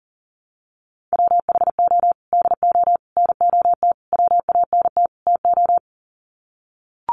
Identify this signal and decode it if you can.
Morse code, heard as a keyed tone, about 40 wpm, 700 Hz. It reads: WHO DO NOT WANT TO
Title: Sets of 5 Words Encore2 40wpm